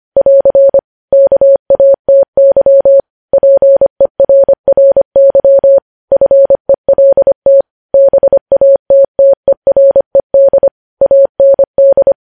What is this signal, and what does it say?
Morse code, heard as a keyed tone, about 25 wpm, 550 Hz. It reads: <AR> KATY PERRY FELT BATTERED AND
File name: podcast / News headlines-1x-25x25